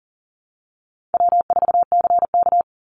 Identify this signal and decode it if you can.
Morse code, heard as a keyed tone, about 37 wpm, 700 Hz. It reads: W4CK
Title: Easy Call Signs Repeated 2x - 40wpm